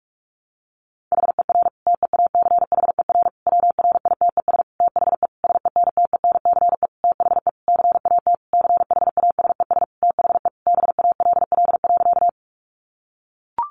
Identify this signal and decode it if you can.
Morse code, heard as a keyed tone, about 44 wpm, 700 Hz. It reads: HER TEACHER WRITES THE SENTENCE THE CAT CHASES THE BALL.
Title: Additional Easy Phrases 45wpm